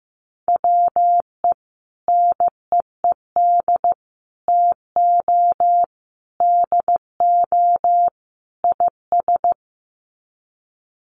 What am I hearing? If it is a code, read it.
Morse code, heard as a keyed tone, about 15 wpm, 700 Hz. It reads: WE NEED TO DO IS
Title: Sets of 5 Words Encore2 15wpm